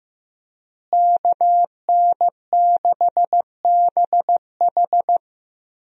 Morse code, heard as a keyed tone, about 15 wpm, 700 Hz. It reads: KN6BH
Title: USA Call Signs - 5 Characters - Repeated 2x - 15wpm